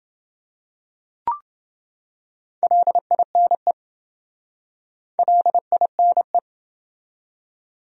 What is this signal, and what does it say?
Morse code, heard as a keyed tone, about 28 wpm, 700 Hz. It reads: LINE LINE